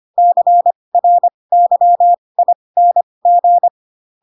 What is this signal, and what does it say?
Morse code, heard as a keyed tone, about 25 wpm, spaced as for 20 wpm, 700 Hz. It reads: CRYING
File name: Pinocchio / Pinocchio ch7 15 25 ik1hge